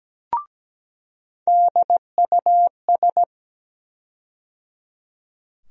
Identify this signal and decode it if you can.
Morse code, heard as a keyed tone, about 17 wpm, 700 Hz. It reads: DUS
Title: USA Call Sign Postfixes - 3 Letters 17wpm